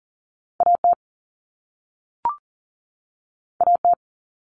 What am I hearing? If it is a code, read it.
Morse code, heard as a keyed tone, about 40 wpm, 700 Hz. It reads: A T A T
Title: Common Cut Numbers - 2-Digit 40wpm